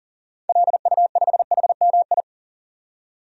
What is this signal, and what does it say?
Morse code, heard as a keyed tone, about 40 wpm, 700 Hz. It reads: LU5HMI
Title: DXCC-100 Call Signs - 6 Characters - 40wpm